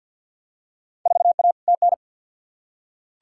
Morse code, heard as a keyed tone, about 47 wpm, 700 Hz. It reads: 4A TN
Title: Field Day Exchanges Encore - 50wpm